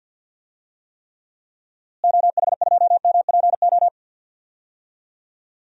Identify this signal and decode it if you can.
Morse code, heard as a keyed tone, about 53 wpm, 700 Hz. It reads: OH2MPO
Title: DXCC-100 Call Signs - 6 Characters - Repeated 3x - 50wpm